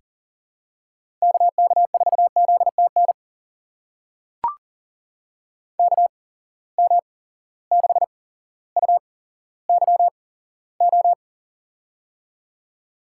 Morse code, heard as a keyed tone, about 40 wpm, spaced as for 17 wpm, 700 Hz. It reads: KK4ZTN KM6UYO
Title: USA Call Signs - 6 Characters - 8x Character Spacing - 40wpm